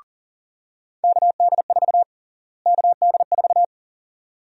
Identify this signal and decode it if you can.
Morse code, heard as a keyed tone, about 40 wpm, 700 Hz. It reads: KD4 KD4
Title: USA Call Sign Prefixes - Repeated 2x - 40wpm